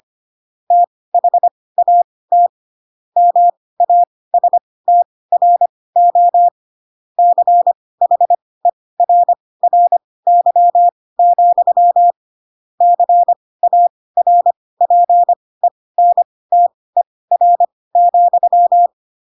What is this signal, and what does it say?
Morse code, heard as a keyed tone, about 25 wpm, spaced as for 18 wpm, 700 Hz. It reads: THAT MASTRO CHERRY, CARPENTER,
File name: Pinocchio / Pinocchio ch1 12 25 ik1hge